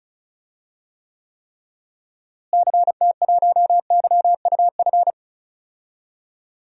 Morse code, heard as a keyed tone, about 36 wpm, 700 Hz. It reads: CT1YUF
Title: DXCC-100 Call Signs - 6 Characters - Repeated 3x - 35wpm